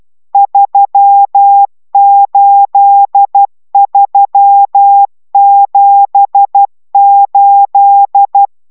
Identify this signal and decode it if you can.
Morse code, heard as a keyed tone, about 12 wpm, 800 Hz. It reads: 38378